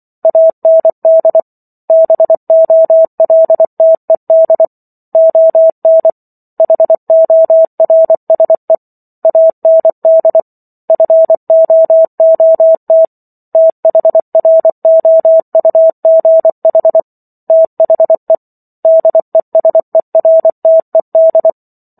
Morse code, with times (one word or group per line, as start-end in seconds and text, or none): 0.2-1.4 AND
1.9-4.7 BOLTED
5.2-6.1 ON
6.6-8.8 HORSE
9.2-10.4 AND
10.9-13.1 FOOT
13.6-17.0 THROUGH
17.5-18.4 THE
18.9-21.5 DESERTED